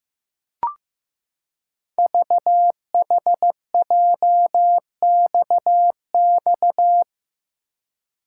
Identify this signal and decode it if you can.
Morse code, heard as a keyed tone, about 15 wpm, 700 Hz. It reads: VHJXX
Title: Code Groups - 5 Letters - 15wpm